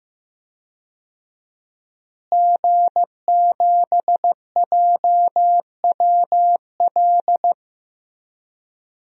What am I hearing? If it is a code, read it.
Morse code, heard as a keyed tone, about 15 wpm, 700 Hz. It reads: G7JWL
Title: DXCC-100 Call Signs - 5 Characters - Repeated 3x - 15wpm